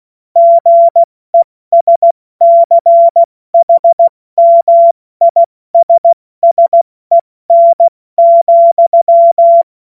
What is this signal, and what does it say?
Morse code, heard as a keyed tone, about 16 wpm, spaced as for 14 wpm, 680 Hz. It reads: GESCHMISSEN,
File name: Morsekurs / Witze / Witz 19. Oktober 2021, 70 F80